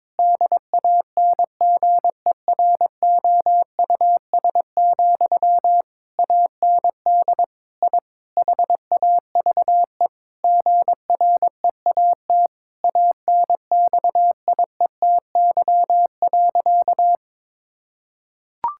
Morse code, heard as a keyed tone, about 22 wpm, 700 Hz. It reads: DANGEROUS, AND I HAVE GREAT ANXIETY.
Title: Sentences from Top 1600 Words 22wpm